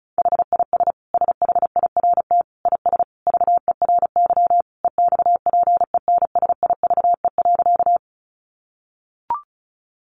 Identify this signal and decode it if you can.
Morse code, heard as a keyed tone, about 34 wpm, 700 Hz. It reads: HIS SHIRT IS VERY EXPENSIVE.
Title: Additional Easy Phrases 35wpm